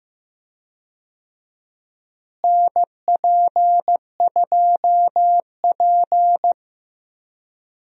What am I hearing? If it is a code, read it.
Morse code, heard as a keyed tone, about 15 wpm, 700 Hz. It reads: NP2P
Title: USA Call Signs - 4 Characters - Repeated 3x - 15wpm